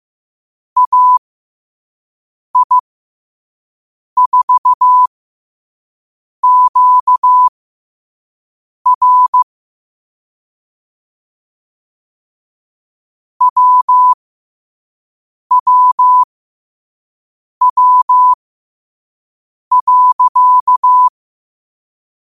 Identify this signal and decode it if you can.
Morse code, heard as a keyed tone, about 15 wpm, spaced as for 5 wpm, 1000 Hz. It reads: AI4QR WWW.
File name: qotdfiles / 5wpm